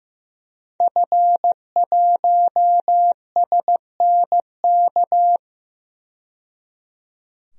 Morse code, heard as a keyed tone, about 15 wpm, 700 Hz. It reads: F1SNK
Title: DXCC-100 Call Signs - 5 Characters - 15wpm